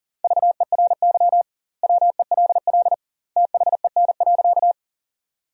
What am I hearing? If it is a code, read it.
Morse code, heard as a keyed tone, about 37 wpm, 700 Hz. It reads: VERY WELL THEN.
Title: 2x Word-Spacing - Sentences from Top 200 Words - 40wpm